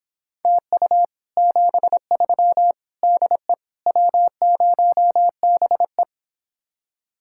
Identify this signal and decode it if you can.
Morse code, heard as a keyed tone, about 26 wpm, 700 Hz. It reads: TU 73 DE W0BE